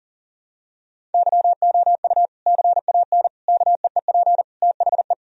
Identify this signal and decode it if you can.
Morse code, heard as a keyed tone, about 40 wpm, 700 Hz. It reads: YOU CAN KEEP THE
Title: Sentences from Top 1000 Words - Review Part 1 - 40wpm